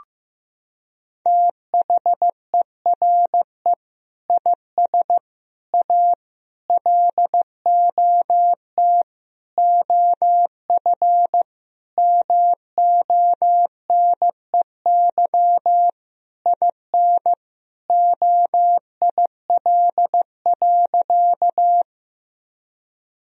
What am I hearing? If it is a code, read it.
Morse code, heard as a keyed tone, about 15 wpm, 700 Hz. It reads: THERE IS A LOT OF MONEY IN OIL.
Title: Sentences from Top 1000 Words - Review Part 4 - 15wpm